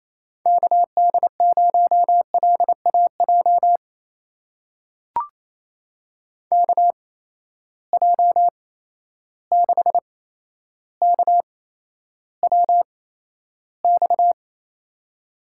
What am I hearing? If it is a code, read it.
Morse code, heard as a keyed tone, about 28 wpm, spaced as for 12 wpm, 700 Hz. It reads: KD0LAJ KJ6KWX
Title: USA Call Signs - 6 Characters - 8x Character Spacing - 28wpm